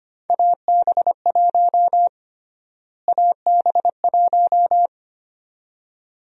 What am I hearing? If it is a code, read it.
Morse code, heard as a keyed tone, about 25 wpm, 700 Hz. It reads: AB1 AB1
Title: USA Call Sign Prefixes - Repeated 2x - 25wpm